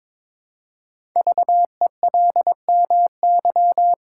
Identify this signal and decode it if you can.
Morse code, heard as a keyed tone, about 22 wpm, 700 Hz. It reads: VELMY
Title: Code Groups - 5 Letters - 22wpm